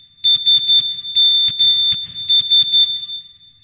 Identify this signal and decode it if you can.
Morse code, heard as a keyed tone, about 11 wpm, 3550 Hz. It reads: SMS
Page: O toque "Morse Sms Sms"